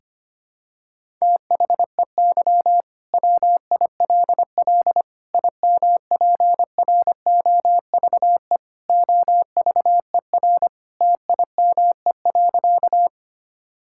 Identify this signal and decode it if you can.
Morse code, heard as a keyed tone, about 25 wpm, 700 Hz. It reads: THEY WILL IMPROVE OVER TIME.
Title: Sentences from Top 1000 Words - Review Part 5 - 25wpm